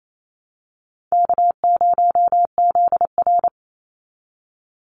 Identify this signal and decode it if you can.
Morse code, heard as a keyed tone, about 28 wpm, 700 Hz. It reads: K0ZR
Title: Call Signs Repeated 3x - 28wpm